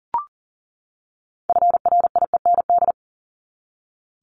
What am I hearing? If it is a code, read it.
Morse code, heard as a keyed tone, about 40 wpm, 700 Hz. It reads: FRIEND
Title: Top 500 Words 40wpm